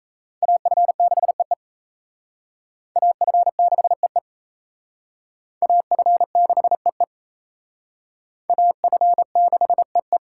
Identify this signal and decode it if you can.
Morse code, heard as a keyed tone, about 35 wpm, 700 Hz. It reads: AF6EE AF6EE AF6EE AF6EE
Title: Speed Racing - USA 2x2 Call Signs 28wpm